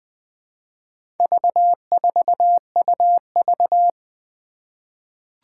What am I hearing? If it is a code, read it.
Morse code, heard as a keyed tone, about 20 wpm, 700 Hz. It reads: V4UV
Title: Four Challenging Letter-Number Combinations 20wpm